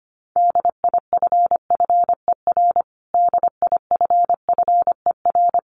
Morse code, heard as a keyed tone, about 25 wpm, 700 Hz. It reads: DIFFER DIFFER
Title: Top 500 Words Repeated 25wpm